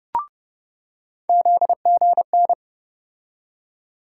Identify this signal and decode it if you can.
Morse code, heard as a keyed tone, about 31 wpm, 700 Hz. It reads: ZGN
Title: USA Call Sign Postfixes - 3 Letters 30wpm